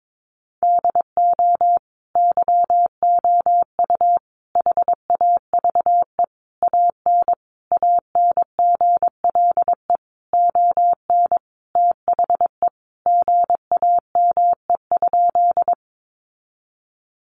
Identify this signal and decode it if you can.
Morse code, heard as a keyed tone, about 22 wpm, 700 Hz. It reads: DO YOU HAVE AN ANGLE ON THE GAME?
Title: Sentences from Top 1500 Words 22wpm